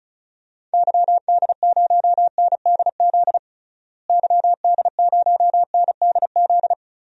Morse code, heard as a keyed tone, about 35 wpm, 700 Hz. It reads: YD0NDZ YD0NDZ
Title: DXCC-100 Call Signs - 6 Characters - Repeated 3x - 35wpm